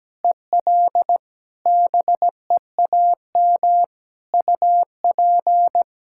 Morse code, heard as a keyed tone, about 17 wpm, 700 Hz. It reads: EL BEAM UP
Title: K1USN SST QSO Practice 17wpm